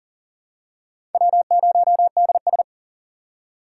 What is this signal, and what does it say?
Morse code, heard as a keyed tone, about 39 wpm, 700 Hz. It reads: W0DS
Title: USA Call Signs - 2x Character Spacing - 40wpm